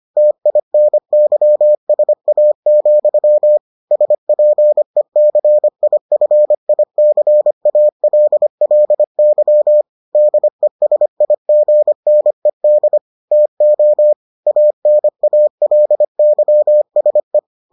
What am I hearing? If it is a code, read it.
Morse code, heard as a keyed tone, about 25 wpm, 600 Hz. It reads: TINYSA, SPECIFICALLY DESIGNED TO ANALYSE